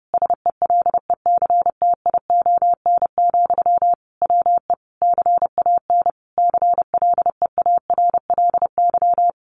Morse code, heard as a keyed tone, about 30 wpm, 700 Hz. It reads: SELECTION, WE CAN CLEARLY
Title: Sentences from Top 1400 Words 30wpm